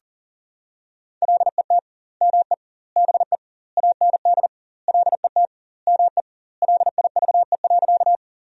Morse code, heard as a keyed tone, about 40 wpm, 700 Hz. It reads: LET ME BE AND LET ME LIVE.
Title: 2x Word-Spacing - Sentences from Top 300 Words - 40wpm